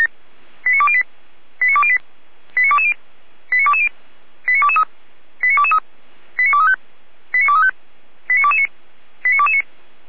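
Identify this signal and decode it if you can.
Five-tone selective call (ZVEI), heard as quick runs of five tones, tones of about 70 ms. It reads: 89208 89208 89220 89220 89223 89223 89235 89235 89229 89229
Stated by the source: ZVEI five-tone selective call received by SDR.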